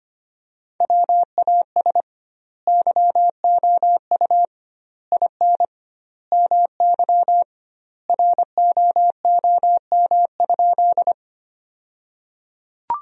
Morse code, heard as a keyed tone, about 25 wpm, 700 Hz. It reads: WAS YOU IN MY ROOM?
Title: 2x Word-Spacing - Sentences from Top 300 Words - 25wpm